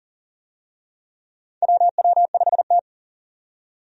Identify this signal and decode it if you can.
Morse code, heard as a keyed tone, about 40 wpm, 700 Hz. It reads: WW5T
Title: USA Call Signs - 4 Characters - 40wpm